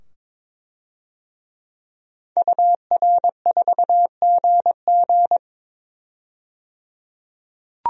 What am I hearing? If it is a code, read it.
Morse code, heard as a keyed tone, about 22 wpm, 700 Hz. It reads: UR4GG
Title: DXCC-100 Call Signs - 5 Characters - 22wpm